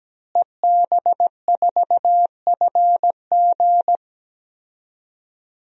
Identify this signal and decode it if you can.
Morse code, heard as a keyed tone, about 17 wpm, 700 Hz. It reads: EB4FG
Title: DXCC-100 Call Signs - 5 Characters - Repeated 3x - 17wpm